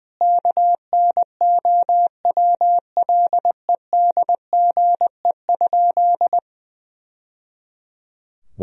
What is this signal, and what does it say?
Morse code, heard as a keyed tone, about 20 wpm, 700 Hz. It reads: KNOWLEDGE?
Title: Sentences from Top 1000 Words - Review Part 3 - 20wpm